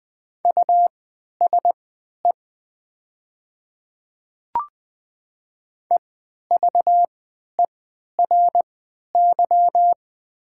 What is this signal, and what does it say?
Morse code, heard as a keyed tone, about 20 wpm, spaced as for 11 wpm, 700 Hz. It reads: USE EVERY
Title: 3x Character Spacing - Top 300 Words - 20wpm